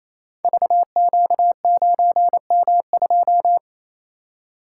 Morse code, heard as a keyed tone, about 28 wpm, 700 Hz. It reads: VQ9M2
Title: Code Groups - 5 Letter:Numbers - 28wpm